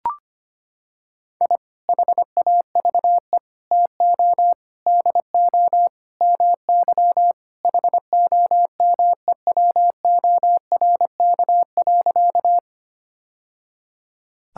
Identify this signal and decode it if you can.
Morse code, heard as a keyed tone, about 25 wpm, 700 Hz. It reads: I HAVE TO DO MY HOMEWORK.
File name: practice / 25wpm / Sentences with 3-Word Phrases - 25wpm